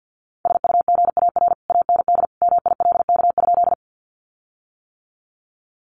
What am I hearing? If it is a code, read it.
Morse code, heard as a keyed tone, about 51 wpm, 700 Hz. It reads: SUGAR AND MILK?
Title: Sentences from Top 1500 Words 50wpm